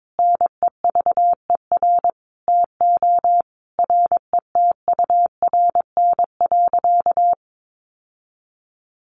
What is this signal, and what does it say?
Morse code, heard as a keyed tone, about 22 wpm, 700 Hz. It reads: NEVER TO RETURN.
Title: Sentences from Top 800 Words 22wpm